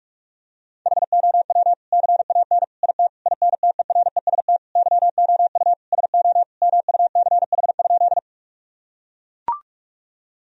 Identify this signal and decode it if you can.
Morse code, heard as a keyed tone, about 41 wpm, 700 Hz. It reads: HOW CAN IT INTEREST YOU SO MUCH?
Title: Sentences from Top 1000 Words - Review Part 2 - 45wpm